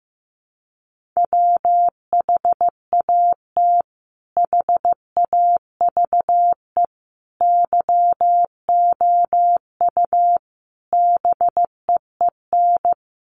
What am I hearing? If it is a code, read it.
Morse code, heard as a keyed tone, about 15 wpm, 700 Hz. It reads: WHAT HAVE YOU BEEN
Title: Sentences from Top 200 Words 15wpm